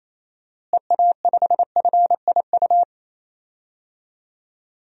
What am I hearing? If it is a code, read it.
Morse code, heard as a keyed tone, about 28 wpm, 700 Hz. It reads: EA5FIU
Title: DXCC-100 Call Signs - 6 Characters - Repeated 2x - 28wpm